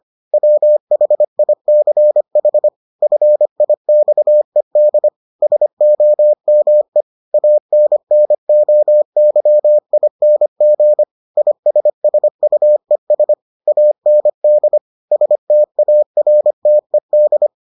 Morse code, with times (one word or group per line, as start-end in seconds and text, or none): 0.3-2.7 WHICH
3.0-5.1 FIXED
5.4-7.0 SOME
7.3-11.0 ANNOYING
11.4-13.3 ISSUES
13.7-14.8 AND
15.1-17.5 STARTED